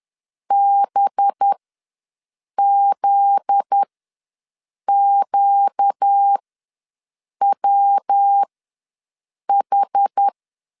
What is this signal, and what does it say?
Morse code, heard as a keyed tone, about 10 wpm, spaced as for 6 wpm, 800 Hz. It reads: BZQWH
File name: LEZIONI / 7 GF9Z / 7 GF9Z ws 10wpm 3919